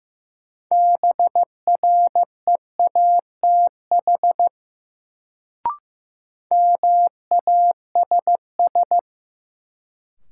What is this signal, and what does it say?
Morse code, heard as a keyed tone, about 15 wpm, 700 Hz. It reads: BREATH MASS
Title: Sets of 5 Words Encore - New Word Drill - 15wpm